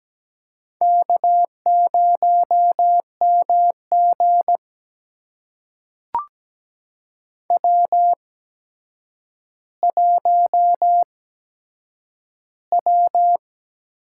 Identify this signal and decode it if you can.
Morse code, heard as a keyed tone, about 17 wpm, spaced as for 8 wpm, 700 Hz. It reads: K0MG W1W
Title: USA Call Signs - 4 Characters - 8x Character Spacing - 17wpm